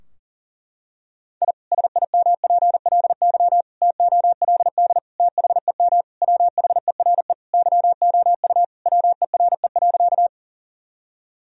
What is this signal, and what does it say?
Morse code, heard as a keyed tone, about 40 wpm, 700 Hz. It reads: I SIMPLY TOLD THEM WHERE YOU WERE.
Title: Sentences from Top 1000 Words - Review Part 4 - 40wpm